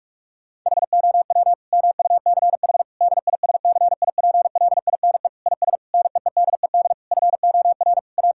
Morse code, heard as a keyed tone, about 41 wpm, 700 Hz. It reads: HOW MUCH DISCIPLINE IS NEEDED FOR A
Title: Sentences from Top 1000 Words - Review Part 6 - 45wpm